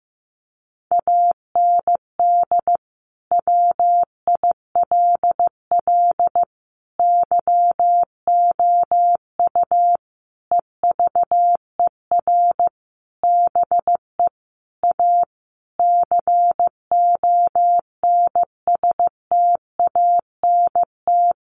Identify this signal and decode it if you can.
Morse code, heard as a keyed tone, about 15 wpm, 700 Hz. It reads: AND WILL YOU EVER BE A CONSTANT
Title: Sentences from Top 1500 Words 15wpm